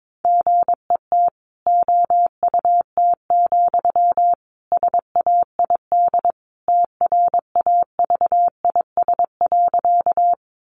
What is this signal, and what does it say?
Morse code, heard as a keyed tone, about 22 wpm, 700 Hz. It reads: GET OUT, SAID TRAVIS.
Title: A Sound Of Thunder 22wpm